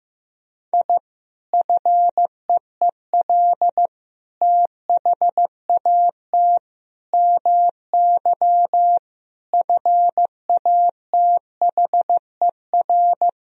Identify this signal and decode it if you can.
Morse code, heard as a keyed tone, about 16 wpm, 700 Hz. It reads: I FEEL THAT MY FATHER
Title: Sentences from Top 1000 Words - Review Part 1 - 15wpm